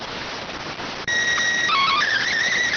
SSTV (slow-scan television), PD 290.